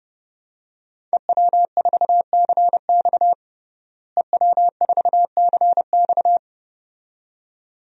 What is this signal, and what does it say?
Morse code, heard as a keyed tone, about 32 wpm, 700 Hz. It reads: EW4CX EW4CX
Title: DXCC-100 Call Signs - 5 Characters - Repeated 2x - 30wpm